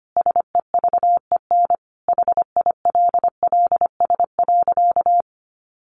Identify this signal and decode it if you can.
Morse code, heard as a keyed tone, about 25 wpm, 700 Hz. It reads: SEVEN HILLS.
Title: A Sound Of Thunder 25wpm